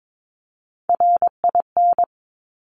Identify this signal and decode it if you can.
Morse code, heard as a keyed tone, about 22 wpm, 700 Hz. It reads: RIN
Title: Top Three Letter Combinations in Words 22wpm